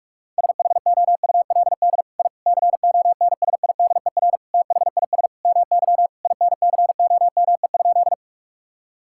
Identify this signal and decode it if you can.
Morse code, heard as a keyed tone, about 41 wpm, 700 Hz. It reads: SHOULD I CONSIDER THIS MY INCOME?
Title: Sentences from Top 1000 Words - Review Part 5 - 45wpm